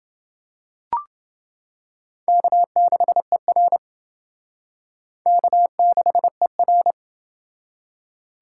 Morse code, heard as a keyed tone, about 28 wpm, 700 Hz. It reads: K6ER K6ER